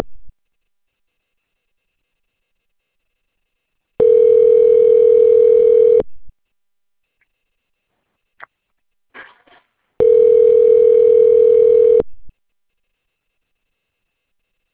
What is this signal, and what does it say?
Ringback tone (North America), 440 + 480 Hz.